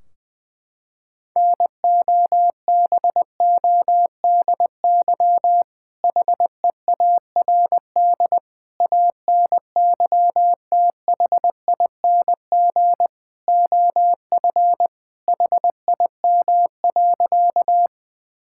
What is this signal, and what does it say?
Morse code, heard as a keyed tone, about 21 wpm, 700 Hz. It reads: NOBODY HEARD ANYTHING OF HIM.
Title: Sentences from Top 1600 Words 20wpm